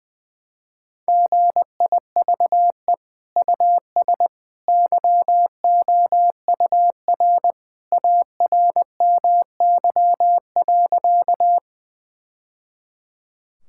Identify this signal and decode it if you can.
Morse code, heard as a keyed tone, about 20 wpm, 700 Hz. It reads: GIVE US YOUR ARMY.
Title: Sentences from Top 1000 Words - Review Part 5 - 20wpm